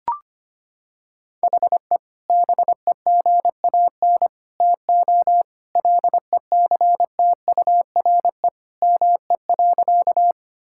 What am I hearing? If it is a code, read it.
Morse code, heard as a keyed tone, about 25 wpm, 700 Hz. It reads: HE BEGAN TO LECTURE ME.
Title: Sentences from Top 1600 Words 25wpm